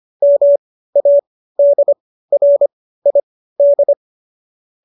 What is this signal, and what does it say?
Morse code, heard as a keyed tone, about 25 wpm, spaced as for 15 wpm, 570 Hz. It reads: MADRID